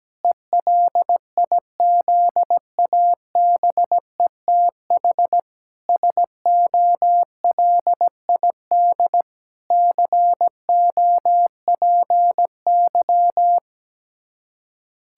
Morse code, heard as a keyed tone, about 17 wpm, 700 Hz. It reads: ELIZABETH SOLID COPY
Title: Practice QSO - Exchange 3 - More Info - 17wpm